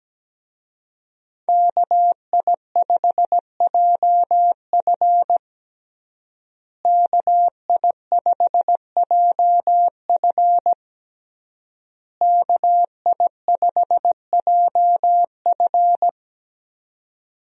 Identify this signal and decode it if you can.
Morse code, heard as a keyed tone, about 17 wpm, 700 Hz. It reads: KI5JF KI5JF KI5JF
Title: USA Call Signs - 5 Characters - Repeated 3x - 17wpm